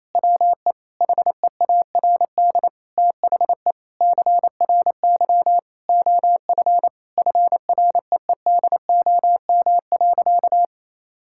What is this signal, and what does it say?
Morse code, heard as a keyed tone, about 28 wpm, 700 Hz. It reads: WE HEARD THE CRY OF FREEDOM.
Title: Sentences from Top 1000 Words - Review Part 5 - 28wpm